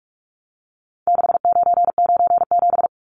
Morse code, heard as a keyed tone, about 44 wpm, 700 Hz. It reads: 6997
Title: Numbers - 4-Digit 45wpm